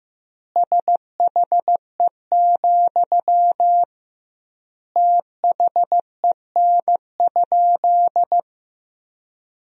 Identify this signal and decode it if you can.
Morse code, heard as a keyed tone, about 15 wpm, 700 Hz. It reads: SHE, THEN?
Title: 2x Word-Spacing - Sentences from Top 200 Words - 15wpm